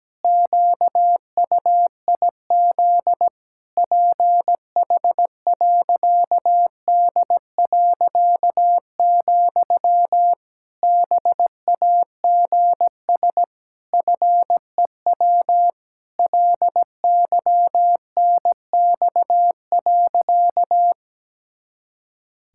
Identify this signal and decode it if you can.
Morse code, heard as a keyed tone, about 17 wpm, 700 Hz. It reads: QUIZ PH.D., BAGS FEW LYNX.
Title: Pangram Challenge v2 - 17wpm